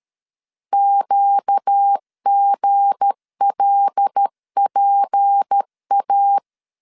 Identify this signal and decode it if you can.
Morse code, heard as a keyed tone, about 13 wpm, 800 Hz. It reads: QGLPA
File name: LEZIONI / 7 GF9Z / 7 G 12wpm 2055